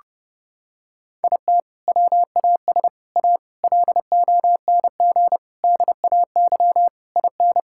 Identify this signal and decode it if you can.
Morse code, heard as a keyed tone, about 30 wpm, 700 Hz. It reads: IT WAS A LONG DAY IN
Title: Sentences from Top 1600 Words 30wpm